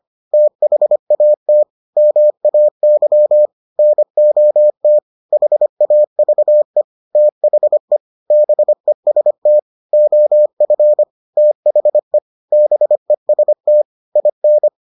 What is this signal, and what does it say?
Morse code, heard as a keyed tone, about 25 wpm, 600 Hz. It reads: THAT MAY NOT HAVE THE BEST OF THE BEST IN